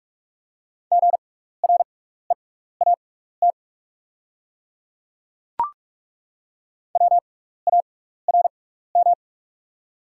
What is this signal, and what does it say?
Morse code, heard as a keyed tone, about 48 wpm, spaced as for 16 wpm, 700 Hz. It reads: GREAT WARM
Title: 6x Character Spacing - Top 500 Words - 45wpm